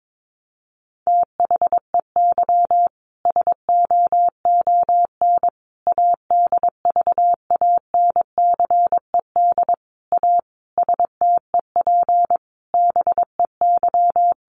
Morse code, heard as a keyed tone, about 22 wpm, 700 Hz. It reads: THEY SOON ADVANCED A STEP BEY
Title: Sentences from Top 1200 Words 22wpm